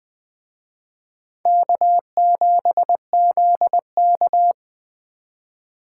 Morse code, heard as a keyed tone, about 21 wpm, 700 Hz. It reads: K7ZK